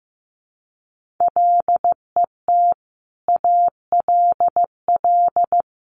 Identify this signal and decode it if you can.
Morse code, heard as a keyed tone, about 15 wpm, 700 Hz. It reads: LET ALL
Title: Sentences from Top 1500 Words 15wpm